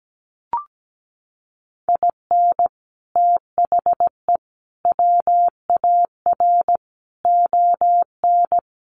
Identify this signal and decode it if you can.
Morse code, heard as a keyed tone, about 17 wpm, 700 Hz. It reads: IN THE WAR ON
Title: Sets of 5 Words Encore2 17wpm